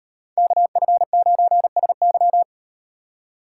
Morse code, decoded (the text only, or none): KF9SY